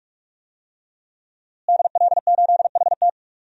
Morse code, heard as a keyed tone, about 46 wpm, 700 Hz. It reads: DL8HT
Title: DXCC-100 Call Signs - 5 Characters - Repeated 3x - 45wpm